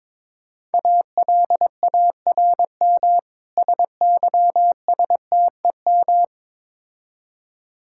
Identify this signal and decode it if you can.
Morse code, heard as a keyed tone, about 22 wpm, 700 Hz. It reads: ALARM SYSTEM
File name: Easy Words - Speed-Racing with Hints - 22wpm